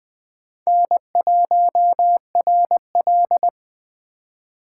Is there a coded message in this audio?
Yes: Morse code from 0.7 to 3.5 s, heard as a keyed tone, about 20 wpm, 700 Hz. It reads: N1RL